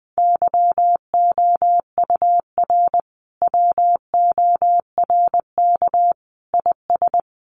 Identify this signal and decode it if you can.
Morse code, heard as a keyed tone, about 20 wpm, 700 Hz. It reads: YOUR WORK IS
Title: Sentences from Top 1100 Words 20wpm